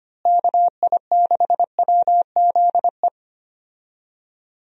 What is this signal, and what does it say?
Morse code, heard as a keyed tone, about 25 wpm, 700 Hz. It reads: KI6WZE
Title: USA Call Signs - 6 Characters - Repeated 2x - 25wpm